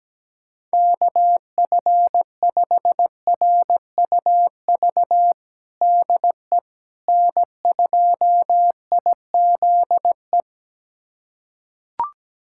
Morse code, heard as a keyed tone, about 17 wpm, 700 Hz. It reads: KF5RUV DE N2IZE
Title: USA Callsign DE Callsign 17wpm